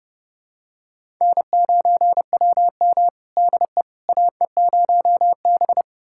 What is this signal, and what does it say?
Morse code, heard as a keyed tone, about 30 wpm, 700 Hz. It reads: N9WM DE AE0B
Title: USA Callsign DE Callsign - 4 Characters - 30wpm